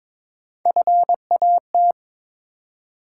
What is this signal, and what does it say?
Morse code, heard as a keyed tone, about 22 wpm, 700 Hz. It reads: FAT
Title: Three Letter Words - 22wpm